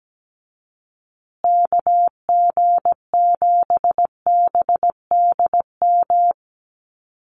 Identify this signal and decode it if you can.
Morse code, heard as a keyed tone, about 17 wpm, 700 Hz. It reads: KG7BDM
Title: USA Call Signs Repeated 2x 17wpm